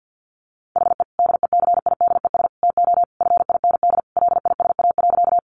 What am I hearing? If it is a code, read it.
Morse code, heard as a keyed tone, about 50 wpm, 700 Hz. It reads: HE DECIDES TO FIND LISA.
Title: Additional Easy Phrases 50wpm